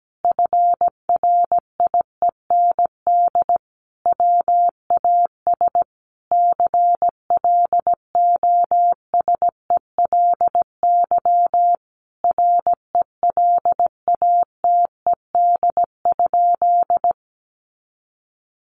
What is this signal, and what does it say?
Morse code, heard as a keyed tone, about 17 wpm, 700 Hz. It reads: FRIEND WAS CLOSELY RELATED?
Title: Sentences from Top 1400 Words 17wpm